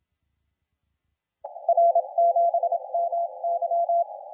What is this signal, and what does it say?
Morse code, heard as a keyed tone, about 25 wpm, 680 Hz. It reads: R7MY
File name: la4c / sac-cw-2017 / CONTEST AUDIO 1100Z